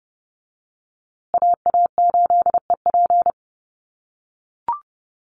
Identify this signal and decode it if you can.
Morse code, heard as a keyed tone, about 30 wpm, 700 Hz. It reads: AA8EP
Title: USA Call Signs Repeated 3x 30wpm